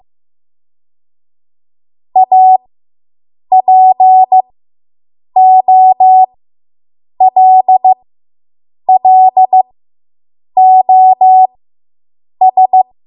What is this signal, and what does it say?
Morse code, heard as a keyed tone, about 15 wpm, spaced as for 7 wpm, 750 Hz. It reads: APOLLOS